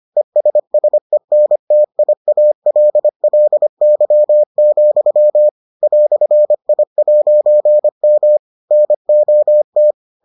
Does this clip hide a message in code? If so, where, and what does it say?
Yes: Morse code from 0.2 to 9.9 s, heard as a keyed tone, about 25 wpm, 600 Hz. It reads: ESSENTIALLY, "I'M NOT